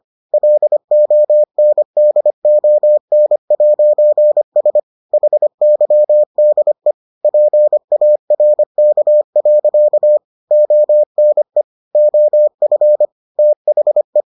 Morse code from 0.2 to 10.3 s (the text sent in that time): LONDON'S HYDE PARK.